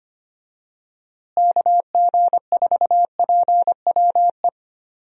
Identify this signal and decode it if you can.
Morse code, heard as a keyed tone, about 25 wpm, 700 Hz. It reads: KG4PWE
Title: USA Call Signs - 6 Characters - Repeated 3x - 25wpm